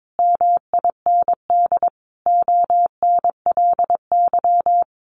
Morse code, heard as a keyed tone, about 22 wpm, 700 Hz. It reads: MIND ONLY
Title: Sentences from Top 800 Words 22wpm